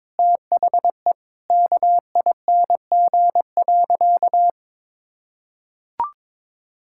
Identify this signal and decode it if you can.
Morse code, heard as a keyed tone, about 21 wpm, 700 Hz. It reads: THE KING.
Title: Sentences from Top 1000 Words - Review Part 6 - 22wpm